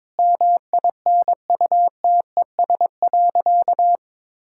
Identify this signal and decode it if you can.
Morse code, heard as a keyed tone, about 22 wpm, 700 Hz. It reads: MINUTES.
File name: practice / 22wpm / Sentences with 3-Word Phrases - 22wpm